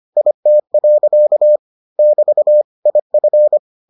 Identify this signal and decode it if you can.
Morse code, heard as a keyed tone, about 25 wpm, 600 Hz. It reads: IT. <BT> IF